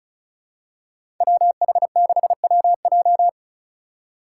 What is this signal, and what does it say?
Morse code, heard as a keyed tone, about 36 wpm, 700 Hz. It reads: WH6WJ
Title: USA Call Signs - 5 Characters - 2x Character Spacing - 35wpm